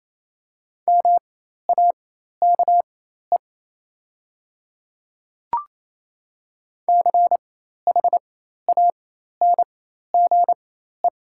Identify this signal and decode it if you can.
Morse code, heard as a keyed tone, about 28 wpm, spaced as for 13 wpm, 700 Hz. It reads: MAKE CHANGE
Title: 4x Character Spacing - Top 200 Words - 28wpm